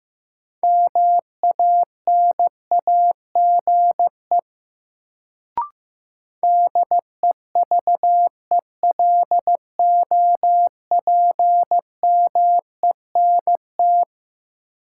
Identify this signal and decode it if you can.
Morse code, heard as a keyed tone, about 15 wpm, 700 Hz. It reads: MANAGE DEVELOPMENT
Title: Sentences from Top 1400 Words - New Word Drill Alt - 15wpm